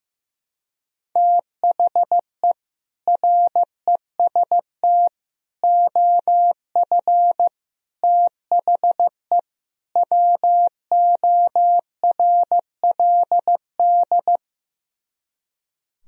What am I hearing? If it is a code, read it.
Morse code, heard as a keyed tone, about 15 wpm, 700 Hz. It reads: THE REST OF THE WORLD
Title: Speed-Racing - Sets of 5 Words 15wpm